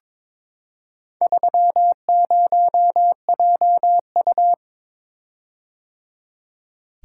Morse code, heard as a keyed tone, about 22 wpm, 700 Hz. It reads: 30JU
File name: practice / 22wpm / Four Letter-Number Combinations - 22wpm